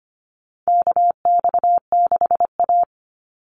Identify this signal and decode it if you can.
Morse code, heard as a keyed tone, about 25 wpm, 700 Hz. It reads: KX6A
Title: NA SOTA Chaser Callsigns 25wpm